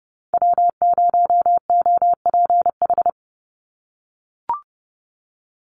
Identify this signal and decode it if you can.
Morse code, heard as a keyed tone, about 30 wpm, 700 Hz. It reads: W0OPH
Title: USA Call Signs 30wpm